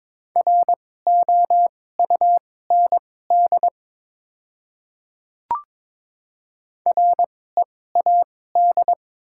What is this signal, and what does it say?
Morse code, heard as a keyed tone, about 22 wpm, spaced as for 16 wpm, 700 Hz. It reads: ROUND READ
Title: 2x Character Spacing - Top 200 Words - 22wpm